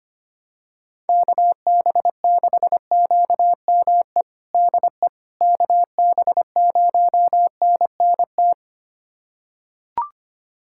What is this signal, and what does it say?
Morse code, heard as a keyed tone, about 25 wpm, 700 Hz. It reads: KB6QME DE KB0NNT
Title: USA Callsign DE Callsign - 6 Characters - 25wpm